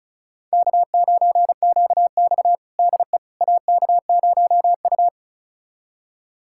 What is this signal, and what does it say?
Morse code, heard as a keyed tone, about 36 wpm, 700 Hz. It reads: K9QX DE AK0U
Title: USA Callsign DE Callsign - 4 Characters - 35wpm